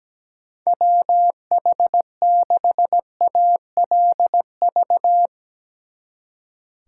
Morse code, heard as a keyed tone, about 18 wpm, 700 Hz. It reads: WH6ALV
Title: Speed Racing - USA 2x3 Call Signs 17wpm